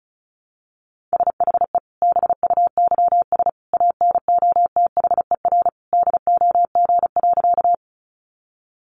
Morse code, heard as a keyed tone, about 34 wpm, 700 Hz. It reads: SHE BUYS ANOTHER DOG.
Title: Easy Phrases 35wpm